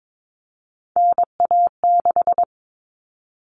Morse code, heard as a keyed tone, about 22 wpm, 700 Hz. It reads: NA6
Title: Common Cut Numbers - 3-Digit 22wpm